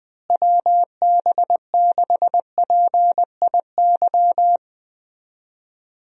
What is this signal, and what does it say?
Morse code, heard as a keyed tone, about 21 wpm, 700 Hz. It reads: WB6PIY
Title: USA Call Signs - 6 Characters - Repeated 2x - 20wpm